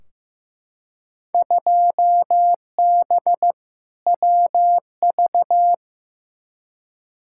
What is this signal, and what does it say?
Morse code, heard as a keyed tone, about 15 wpm, 700 Hz. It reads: 2B WV
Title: Field Day Exchanges Encore - 15wpm